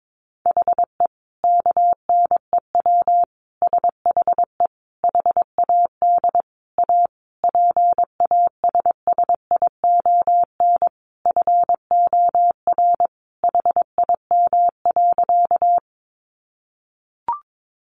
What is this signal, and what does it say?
Morse code, heard as a keyed tone, about 22 wpm, 700 Hz. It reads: HE KNEW SHE HAD A PASSION FOR HIM.
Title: Sentences from Top 1300 Words 22wpm